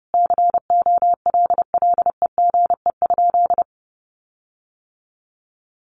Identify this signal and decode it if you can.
Morse code, heard as a keyed tone, about 30 wpm, 700 Hz. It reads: COLLEGE?
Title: Sentences from Top 1300 Words 30wpm